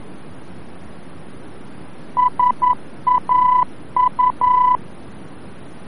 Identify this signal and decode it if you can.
Morse code, heard as a keyed tone, about 11 wpm, 1000 Hz. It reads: SAU
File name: france / mx / SAU-VOR-F